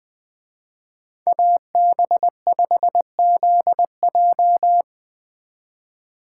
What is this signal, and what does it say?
Morse code, heard as a keyed tone, about 19 wpm, 700 Hz. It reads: AB5ZJ
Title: Speed Racing - USA 2x2 Call Signs 17wpm